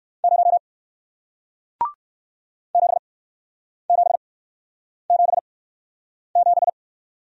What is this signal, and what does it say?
Morse code, heard as a keyed tone, about 56 wpm, spaced as for 9 wpm, 700 Hz. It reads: 0 7777
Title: Speed-Racing Single Letter-Number 45wpm